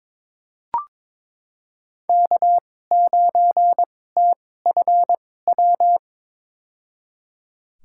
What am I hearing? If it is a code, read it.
Morse code, heard as a keyed tone, about 22 wpm, spaced as for 16 wpm, 700 Hz. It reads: K9TFW
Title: USA Call Signs - 5 Characters - 2x Character Spacing - 22wpm